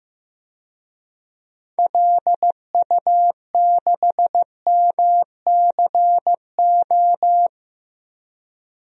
Morse code, heard as a keyed tone, about 15 wpm, 700 Hz. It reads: LU6MCO
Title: DXCC-100 Call Signs - 6 Characters - Repeated 2x - 15wpm